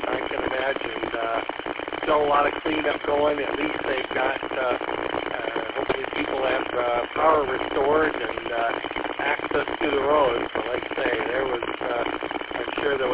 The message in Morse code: none

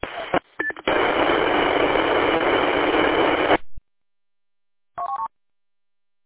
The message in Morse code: none

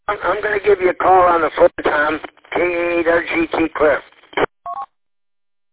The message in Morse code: none

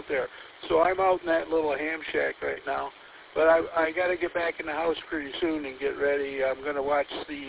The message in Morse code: none

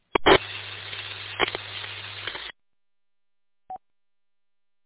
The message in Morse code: none

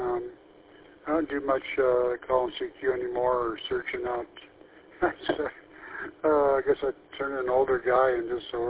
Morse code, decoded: none

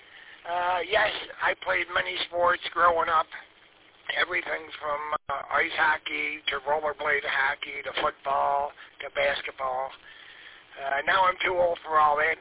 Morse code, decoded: none

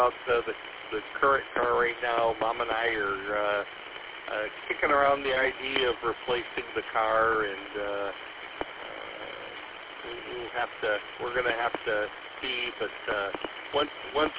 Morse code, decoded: none

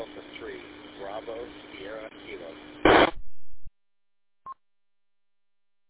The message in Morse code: none

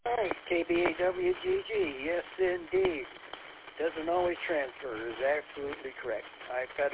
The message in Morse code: none

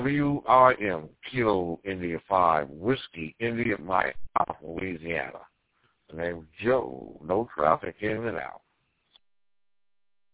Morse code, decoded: none

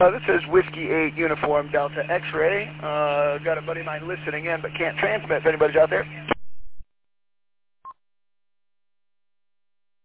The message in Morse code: none

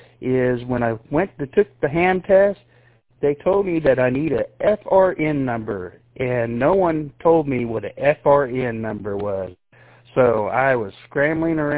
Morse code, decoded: none

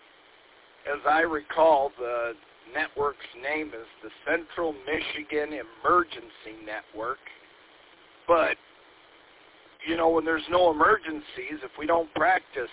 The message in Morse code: none

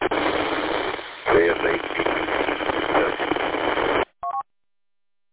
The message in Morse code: none

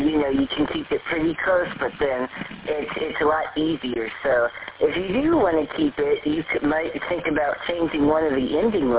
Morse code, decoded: none

